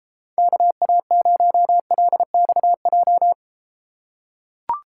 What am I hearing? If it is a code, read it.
Morse code, heard as a keyed tone, about 32 wpm, 700 Hz. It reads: KA0LXJ